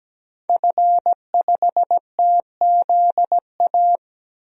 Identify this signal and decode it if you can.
Morse code, heard as a keyed tone, about 17 wpm, 700 Hz. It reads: F5TZA
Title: DXCC-100 Call Signs - 5 Characters - 17wpm